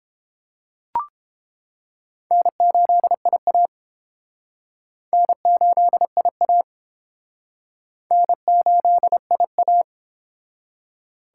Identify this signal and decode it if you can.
Morse code, heard as a keyed tone, about 30 wpm, 700 Hz. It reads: N8IA N8IA N8IA